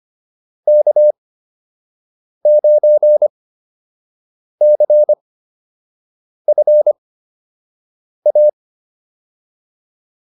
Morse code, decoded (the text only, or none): K9CFA